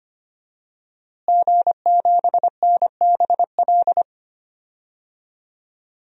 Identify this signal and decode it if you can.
Morse code, heard as a keyed tone, about 25 wpm, 700 Hz. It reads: G7NBL
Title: DXCC-100 Call Signs - 5 Characters - Repeated 2x - 25wpm